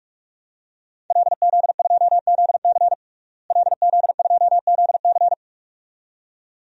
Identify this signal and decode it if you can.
Morse code, heard as a keyed tone, about 43 wpm, 700 Hz. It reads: LZ2ZC LZ2ZC
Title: DXCC-100 Call Signs - 5 Characters - Repeated 2x - 45wpm